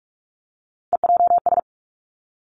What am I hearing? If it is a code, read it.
Morse code, heard as a keyed tone, about 46 wpm, 700 Hz. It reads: EJS
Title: Three Challenging Letter-Number Combinations 45wpm